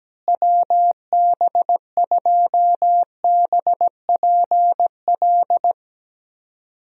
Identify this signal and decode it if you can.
Morse code, heard as a keyed tone, about 17 wpm, 700 Hz. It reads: WB2BPL
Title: USA Call Signs - 6 Characters - Repeated 3x - 17wpm